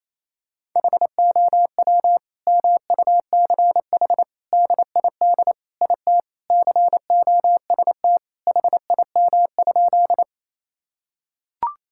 Morse code, heard as a keyed tone, about 28 wpm, 700 Hz. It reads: HOW MUCH DID IT COST HIM?
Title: Sentences from Top 1000 Words - Review Part 5 - 28wpm